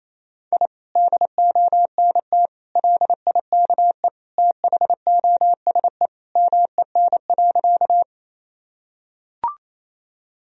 Morse code, heard as a keyed tone, about 28 wpm, 700 Hz. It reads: I DONT LIKE THOSE MEN.
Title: Sentences from Top 1000 Words - Review Part 1 - 28wpm